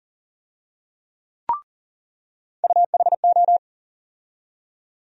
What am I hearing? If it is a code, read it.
Morse code, heard as a keyed tone, about 40 wpm, 700 Hz. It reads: UHO
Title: USA Call Sign Postfixes - 3 Letters 40wpm